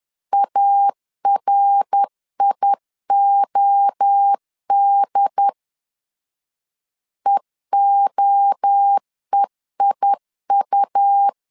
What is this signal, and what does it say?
Morse code, heard as a keyed tone, about 10 wpm, 800 Hz. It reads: ARIOD EOEIU
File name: CW / DET / 1 3 1500C S C 10wpm 3335